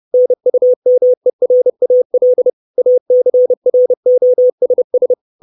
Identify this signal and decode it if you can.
Morse code, heard as a keyed tone, about 30 wpm, 500 Hz. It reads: NUMERAL ACROSS
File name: 632 Words 30WPM